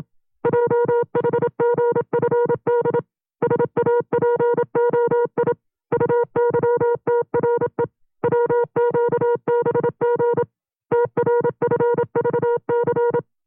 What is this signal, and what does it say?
Morse code, heard as a keyed tone, about 27 wpm, 480 Hz. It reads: JHGFD SAPOI UYTRE WQBG TRFVC
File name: CW-practice/25WPM